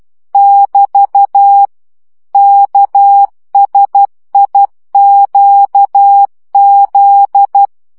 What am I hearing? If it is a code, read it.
Morse code, heard as a keyed tone, about 12 wpm, 800 Hz. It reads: <BT> KSIQZ
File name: morze vizsga 2011-2 10